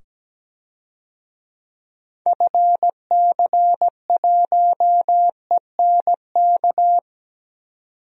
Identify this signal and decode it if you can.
Morse code, heard as a keyed tone, about 17 wpm, 700 Hz. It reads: FC1ENK
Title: DXCC-100 Call Signs - 6 Characters - 17wpm